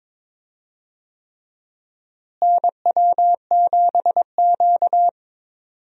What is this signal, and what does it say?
Morse code, heard as a keyed tone, about 22 wpm, 700 Hz. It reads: NW7Q
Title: USA Call Signs - 4 Characters - Repeated 3x - 22wpm